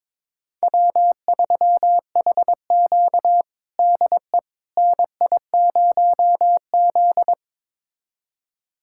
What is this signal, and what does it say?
Morse code, heard as a keyed tone, about 21 wpm, 700 Hz. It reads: W3HQ DE NI0Z
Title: USA Callsign DE Callsign - 4 Characters - 22wpm